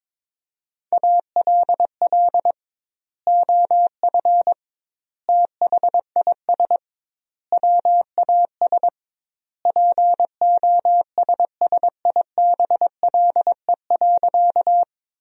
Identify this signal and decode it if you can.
Morse code, heard as a keyed tone, about 22 wpm, 700 Hz. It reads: ALL OF THIS WAS POSSIBLE.
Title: 2x Word-Spacing - Sentences from Top 500 Words - 22wpm